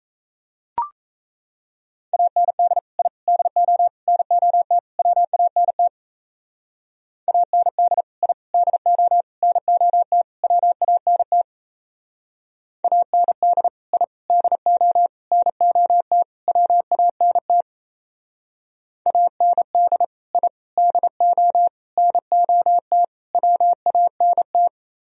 Morse code, heard as a keyed tone, about 35 wpm, 700 Hz. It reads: AND I DO NOT WANT AND I DO NOT WANT AND I DO NOT WANT AND I DO NOT WANT
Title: Speed-Racing - Sets of 5 Words 28wpm